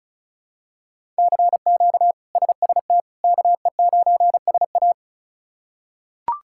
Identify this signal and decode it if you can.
Morse code, heard as a keyed tone, about 36 wpm, 700 Hz. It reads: CQ SST KE9SA
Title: K1USN SST QSO Practice - 35wpm